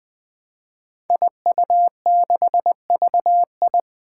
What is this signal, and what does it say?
Morse code, heard as a keyed tone, about 20 wpm, 700 Hz. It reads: IU6VI
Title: Five Challenging Letter-Number Combinations 20wpm